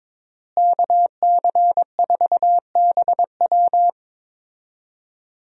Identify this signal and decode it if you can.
Morse code, heard as a keyed tone, about 22 wpm, 700 Hz. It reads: KC4BW
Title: USA Call Signs - 5 Characters - Repeated 3x - 22wpm